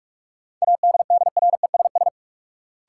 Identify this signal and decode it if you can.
Morse code, heard as a keyed tone, about 47 wpm, 700 Hz. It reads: ADDRESS
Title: Sets of 5 Words Encore - New Word Drill - 45wpm